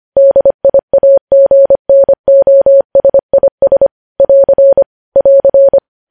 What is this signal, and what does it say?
Morse code, heard as a keyed tone, about 25 wpm, 550 Hz. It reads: DIAGNOSIS <AR> <AR>
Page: news-headlines-in-morse-code-at-25-wpm